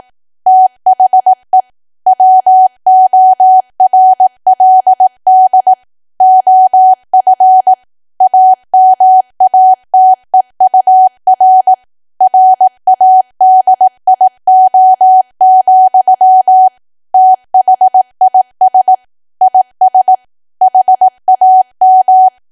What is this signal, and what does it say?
Morse code, heard as a keyed tone, about 18 wpm, 750 Hz. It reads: THE WORLD OF AMATEUR RADIO, THIS IS HAM
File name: Archive / 18 WPM / 18WPM